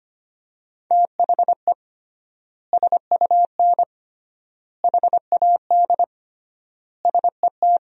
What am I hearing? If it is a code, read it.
Morse code, heard as a keyed tone, about 25 wpm, 700 Hz. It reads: THE SUN HAD SET
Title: 3x Word-Spacing - Sentences from Top 300 Words - 25wpm